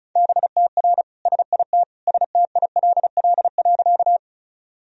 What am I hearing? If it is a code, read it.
Morse code, heard as a keyed tone, about 36 wpm, 700 Hz. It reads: BTR SIT STILL.
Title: Sentences from Top 500 Words and Abbreviations 35wpm